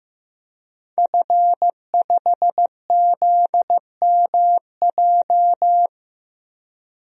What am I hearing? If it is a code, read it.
Morse code, heard as a keyed tone, about 15 wpm, 700 Hz. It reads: F5ZMJ
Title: DXCC-100 Call Signs - 5 Characters - 15wpm